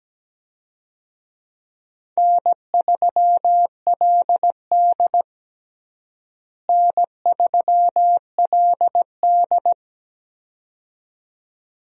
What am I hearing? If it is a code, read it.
Morse code, heard as a keyed tone, about 17 wpm, 700 Hz. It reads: N3LD N3LD
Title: USA Call Signs - 4 Characters - Repeated 2x - 17wpm